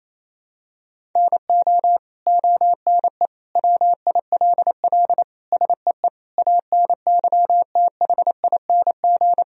Morse code, heard as a keyed tone, about 28 wpm, 700 Hz. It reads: NO ONE WILL SEE ANYTHING
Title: Sentences from Top 1000 Words - Review Part 3 - 28wpm